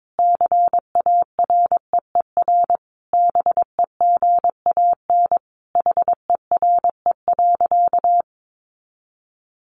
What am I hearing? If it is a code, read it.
Morse code, heard as a keyed tone, about 22 wpm, 700 Hz. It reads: CAREER BEGAN HERE.
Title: Sentences from Top 1100 Words 22wpm